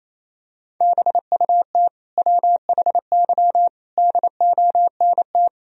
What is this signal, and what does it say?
Morse code, heard as a keyed tone, about 28 wpm, 700 Hz. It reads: BUT WHY DONT
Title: Sentences from Top 300 Words and Abbreviations 28wpm